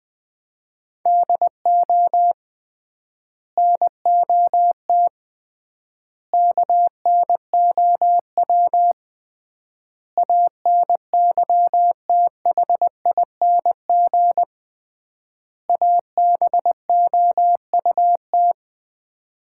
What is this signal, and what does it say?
Morse code, heard as a keyed tone, about 20 wpm, 700 Hz. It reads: DO NOT KNOW ANYTHING ABOUT
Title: 3x Word-Spacing - Sets of 5 Words - 20wpm